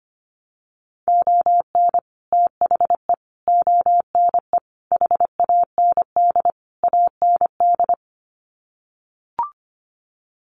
Morse code, heard as a keyed tone, about 25 wpm, 700 Hz. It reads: ON THE ONE HAND AND
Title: Sets of 5 Words Encore2 25wpm